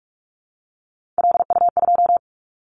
Morse code, heard as a keyed tone, about 46 wpm, 700 Hz. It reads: LU2
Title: Callsign Prefixes for Top 100 DXCC - 45wpm